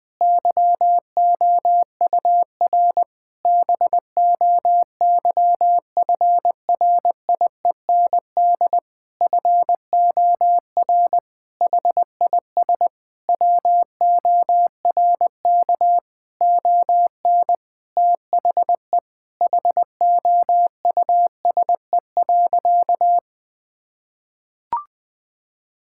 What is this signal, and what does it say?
Morse code, heard as a keyed tone, about 21 wpm, 700 Hz. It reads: YOUR BOYFRIEND FOR HIS WORK ON THE HOUSE.
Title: Sentences from Top 1000 Words - Review Part 5 - 20wpm